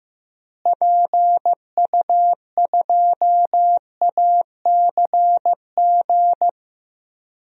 Morse code, heard as a keyed tone, about 15 wpm, 700 Hz. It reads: PU2ACG
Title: DXCC-100 Call Signs - 6 Characters - Repeated 2x - 15wpm